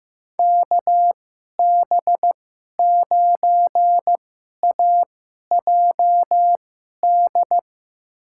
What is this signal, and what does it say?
Morse code, heard as a keyed tone, about 15 wpm, spaced as for 11 wpm, 700 Hz. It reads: KB9AJD
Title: USA Call Signs - 6 Characters - 2x Character Spacing - 15wpm